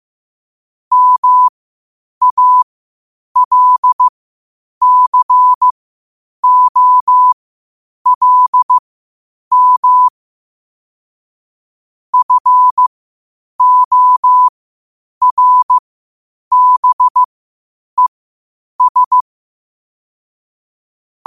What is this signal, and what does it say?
Morse code, heard as a keyed tone, about 15 wpm, spaced as for 8 wpm, 1000 Hz. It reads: MALCOLM FORBES
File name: qotdfiles / 8wpm